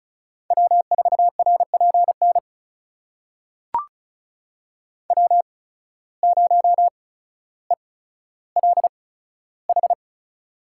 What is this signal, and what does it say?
Morse code, heard as a keyed tone, about 34 wpm, spaced as for 15 wpm, 700 Hz. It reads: W4RPN W0ELH
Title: USA Call Signs - 5 Characters - 8x Character Spacing - 35wpm